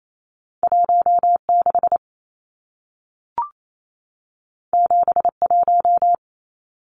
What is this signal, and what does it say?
Morse code, heard as a keyed tone, about 28 wpm, 700 Hz. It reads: 16 71
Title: Numbers - 2-Digit 28wpm